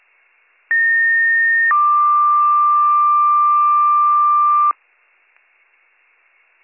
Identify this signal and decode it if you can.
Two-tone page, heard as a one-second tone, then a three-second tone, about 1850 then 1180 Hz. It is heard from 0.7 to 4.7 s.